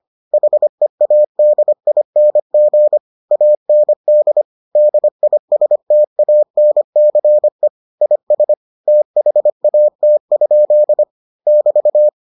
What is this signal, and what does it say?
Morse code, heard as a keyed tone, about 25 wpm, 600 Hz. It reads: HEADING AND DISTANCE IS THAT? <BT>